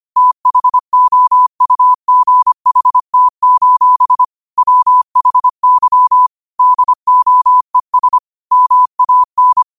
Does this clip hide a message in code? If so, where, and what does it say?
Yes: Morse code from 0.2 to 9.6 s, heard as a keyed tone, about 25 wpm, 1000 Hz. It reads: THOUGHT: WHY DOES MAN